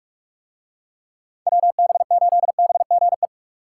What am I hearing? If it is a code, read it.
Morse code, heard as a keyed tone, about 45 wpm, 700 Hz. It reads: WB8BGE